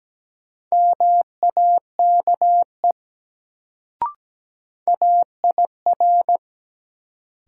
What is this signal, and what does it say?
Morse code, heard as a keyed tone, about 17 wpm, 700 Hz. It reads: MAKE AIR
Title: Sets of 2 Words - New Word Drill - 17wpm